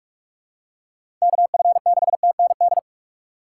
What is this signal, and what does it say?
Morse code, heard as a keyed tone, about 46 wpm, 700 Hz. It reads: KF6TND
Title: USA Call Signs - 6 Characters - 6x Character Spacing - 45wpm